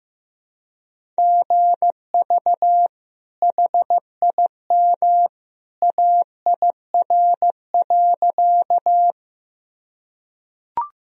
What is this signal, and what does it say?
Morse code, heard as a keyed tone, about 16 wpm, 700 Hz. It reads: GV HIM AIR.
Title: Sentences from Top 200 Words and Abbreviations 15wpm